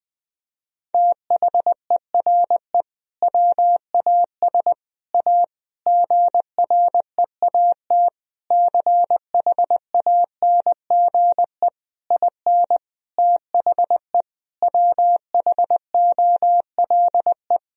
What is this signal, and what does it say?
Morse code, heard as a keyed tone, about 20 wpm, 700 Hz. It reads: THERE WAS A GREAT CHANGE IN THE WHOLE
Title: Sentences from Top 1000 Words - Review Part 3 - 20wpm